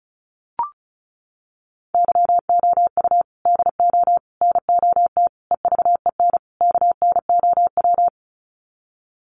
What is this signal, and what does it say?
Morse code, heard as a keyed tone, about 34 wpm, 700 Hz. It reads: YOU DO NOT EVEN KNOW
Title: Sets of 5 Words Encore2 35wpm